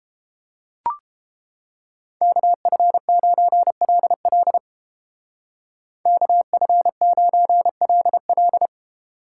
Morse code, heard as a keyed tone, about 29 wpm, 700 Hz. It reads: KF9LL KF9LL